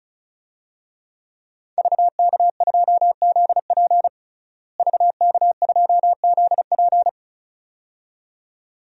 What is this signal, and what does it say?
Morse code, heard as a keyed tone, about 34 wpm, 700 Hz. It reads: VK2ZP VK2ZP
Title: DXCC-100 Call Signs - 5 Characters - Repeated 2x - 35wpm